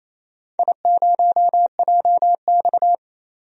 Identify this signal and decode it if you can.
Morse code, heard as a keyed tone, about 28 wpm, 700 Hz. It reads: I0JX
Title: DXCC-100 Call Signs - 4 Characters - 28wpm